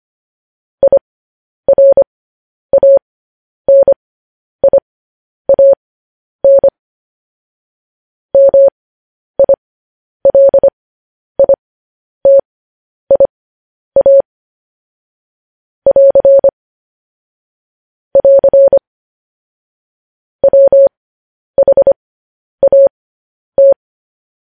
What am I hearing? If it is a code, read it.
Morse code, heard as a keyed tone, about 25 wpm, spaced as for 10 wpm, 550 Hz. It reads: IRANIAN MILITIA <AR> <AR> WHAT